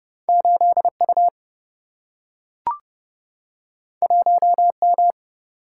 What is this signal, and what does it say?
Morse code, heard as a keyed tone, about 30 wpm, 700 Hz. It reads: 8U 1M
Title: Single-Letter Single-Number 30wpm